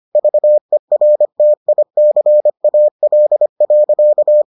Morse code, heard as a keyed tone, about 25 wpm, 600 Hz. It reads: VERTICAL.